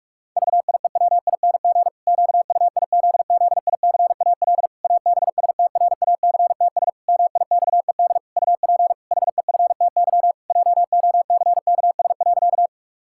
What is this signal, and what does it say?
Morse code, heard as a keyed tone, about 41 wpm, 700 Hz. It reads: VIEWING QUIZZICAL ABSTRACTS MIXED UP HEFTY JOCKS.
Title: Pangram Challenge v2 - 45wpm